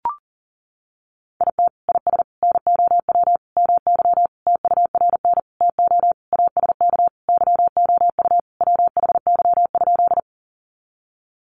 Morse code, heard as a keyed tone, about 40 wpm, 700 Hz. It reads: IT IS NOW MY TURN TO ASK YOU WHY?
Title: Sentences from Top 200 Words 40wpm